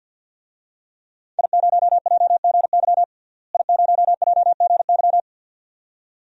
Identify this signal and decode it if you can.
Morse code, heard as a keyed tone, about 53 wpm, 700 Hz. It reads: I0JGY I0JGY
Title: DXCC-100 Call Signs - 5 Characters - Repeated 2x - 50wpm